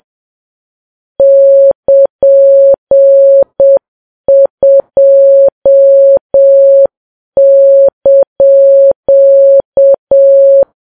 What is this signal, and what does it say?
Morse code, heard as a keyed tone, about 7 wpm, 550 Hz. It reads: <KN>2)
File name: Files Web tls / Morse on9fv / morse7kps